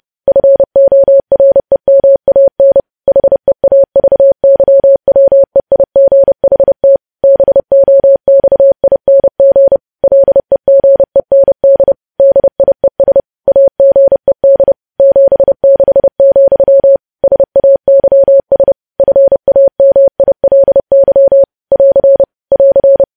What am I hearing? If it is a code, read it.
Morse code, heard as a keyed tone, about 30 wpm, 550 Hz. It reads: FOREMAN HEAVYWEIGHT BOXING LEGEND DIES AGED 76, SAYS FAMILY <AR> <AR>
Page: news headlines in morse code at 30 wpm